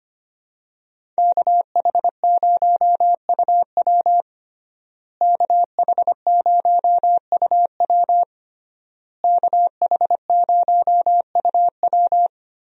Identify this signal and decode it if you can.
Morse code, heard as a keyed tone, about 25 wpm, 700 Hz. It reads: KH0UW KH0UW KH0UW
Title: USA Call Signs - 5 Characters - Repeated 3x - 25wpm